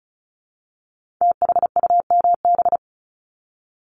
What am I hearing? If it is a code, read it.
Morse code, heard as a keyed tone, about 36 wpm, 700 Hz. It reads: THUMB
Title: Multi-letter Phonemes 35wpm